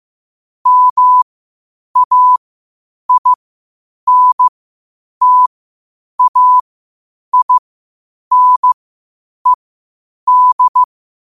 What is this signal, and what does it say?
Morse code, heard as a keyed tone, about 15 wpm, spaced as for 8 wpm, 1000 Hz. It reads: MAINTAINED
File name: qotdfiles / 8wpm